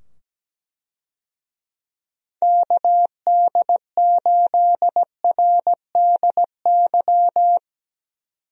Morse code, heard as a keyed tone, about 17 wpm, 700 Hz. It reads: KD8RDY
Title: USA Call Signs - 6 Characters - 17wpm